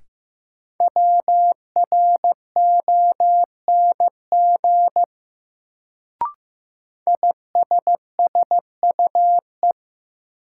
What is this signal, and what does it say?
Morse code, heard as a keyed tone, about 15 wpm, 700 Hz. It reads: WRONG ISSUE
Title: Sentences from Top 900 Words - New Word Drill - 15wpm